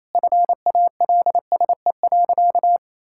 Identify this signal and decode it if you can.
Morse code, heard as a keyed tone, about 28 wpm, 700 Hz. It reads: FALSE.
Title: Sentences from Top 1600 Words 28wpm